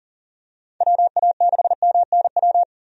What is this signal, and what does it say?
Morse code, heard as a keyed tone, about 40 wpm, 700 Hz. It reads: WA6MNW
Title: USA Call Signs - 6 Characters - 40wpm